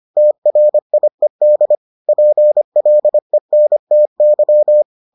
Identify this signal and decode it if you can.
Morse code, heard as a keyed tone, about 25 wpm, 600 Hz. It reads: TRIED PLENTY